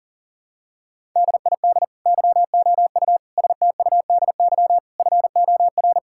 Morse code, heard as a keyed tone, about 37 wpm, 700 Hz. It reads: DID YOU STUDY FOR
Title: Sentences from Top 1000 Words - Review Part 6 - 40wpm